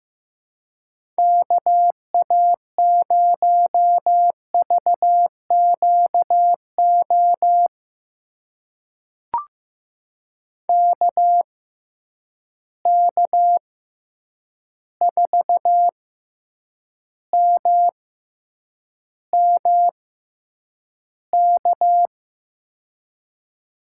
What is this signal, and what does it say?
Morse code, heard as a keyed tone, about 15 wpm, spaced as for 8 wpm, 700 Hz. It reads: KA0VQO KK4MMK
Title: USA Call Signs - 6x Character Spacing - 15wpm